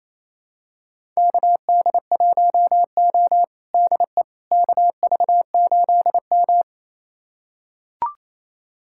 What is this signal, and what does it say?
Morse code, heard as a keyed tone, about 28 wpm, 700 Hz. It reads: KD1O DE KV8M
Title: USA Callsign DE Callsign - 4 Characters - 28wpm